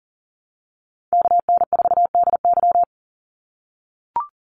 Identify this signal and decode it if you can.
Morse code, heard as a keyed tone, about 40 wpm, 700 Hz. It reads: KN4DY
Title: USA Call Signs 40wpm